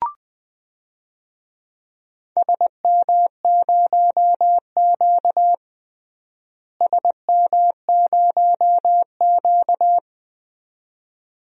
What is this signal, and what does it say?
Morse code, heard as a keyed tone, about 20 wpm, 700 Hz. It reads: SM0Q SM0Q
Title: DXCC-100 Call Signs - 4 Characters - Repeated 2x - 20wpm